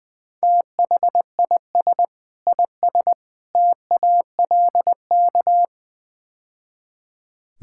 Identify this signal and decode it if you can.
Morse code, heard as a keyed tone, about 20 wpm, 700 Hz. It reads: THIS IS TALK
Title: Build A Phrase - 5-Word Phrases - 20wpm